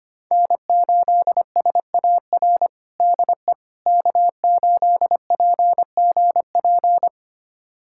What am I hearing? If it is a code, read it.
Morse code, heard as a keyed tone, about 25 wpm, 700 Hz. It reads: N8SAR DE K8PGP
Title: USA Callsign DE Callsign - 5 Characters - 25wpm